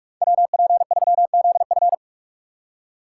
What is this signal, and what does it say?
Morse code, heard as a keyed tone, about 41 wpm, 700 Hz. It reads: WP3ZF
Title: USA Call Signs - 5 Characters - 2x Character Spacing - 45wpm